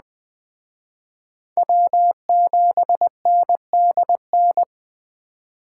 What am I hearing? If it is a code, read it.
Morse code, heard as a keyed tone, about 21 wpm, 700 Hz. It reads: W7NDN
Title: Speed Racing - USA Call Signs 17wpm